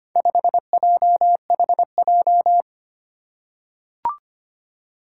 Morse code, heard as a keyed tone, about 25 wpm, 700 Hz. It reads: 5JHJ
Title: Four Challenging Letter-Number Combinations 25wpm